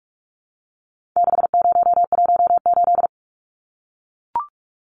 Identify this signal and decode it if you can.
Morse code, heard as a keyed tone, about 44 wpm, 700 Hz. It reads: 6018
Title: Numbers - 4-Digit 45wpm